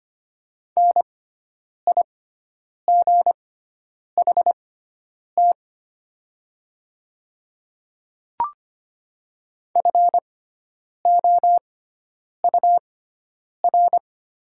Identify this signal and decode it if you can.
Morse code, heard as a keyed tone, about 25 wpm, spaced as for 9 wpm, 700 Hz. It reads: NIGHT FOUR
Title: 6x Character Spacing - Top 300 Words - 25wpm